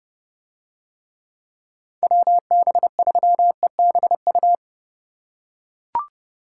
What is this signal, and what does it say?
Morse code, heard as a keyed tone, about 29 wpm, 700 Hz. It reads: WB3EBU